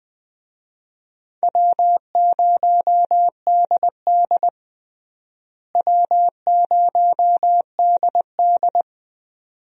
Morse code, heard as a keyed tone, about 20 wpm, 700 Hz. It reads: W0DD W0DD
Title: USA Call Signs - 4 Characters - Repeated 2x - 20wpm